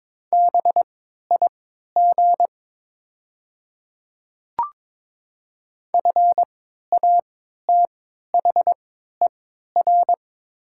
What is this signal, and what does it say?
Morse code, heard as a keyed tone, about 21 wpm, spaced as for 12 wpm, 700 Hz. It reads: BIG FATHER
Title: 3x Character Spacing - Top 200 Words - 22wpm